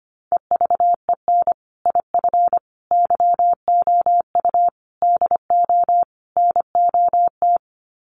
Morse code, heard as a keyed tone, about 25 wpm, 700 Hz. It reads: EVEN IF YOU DO NOT
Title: Sets of 5 Words Encore 25wpm